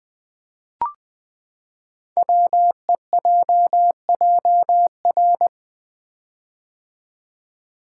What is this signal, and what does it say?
Morse code, heard as a keyed tone, about 20 wpm, 700 Hz. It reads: WEJJR
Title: Code Groups - 5 Letters - 20wpm